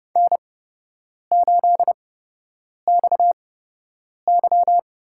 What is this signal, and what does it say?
Morse code, heard as a keyed tone, about 32 wpm, spaced as for 8 wpm, 700 Hz. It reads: N8XY
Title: USA Call Signs - 4 Characters - 8x Character Spacing - 30wpm